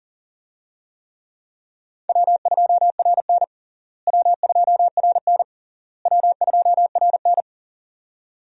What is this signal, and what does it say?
Morse code, heard as a keyed tone, about 40 wpm, 700 Hz. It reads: W2RN W2RN W2RN
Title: USA Call Signs - 4 Characters - Repeated 3x - 40wpm